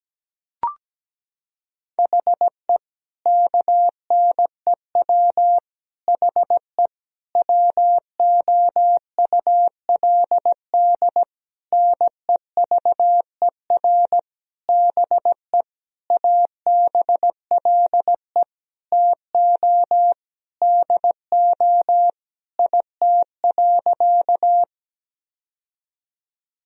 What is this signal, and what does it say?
Morse code, heard as a keyed tone, about 17 wpm, 700 Hz. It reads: HE KNEW HE WOULD NEVER BE ABLE TO DO IT.
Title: Sentences from Top 1000 Words - Review Part 2 - 17wpm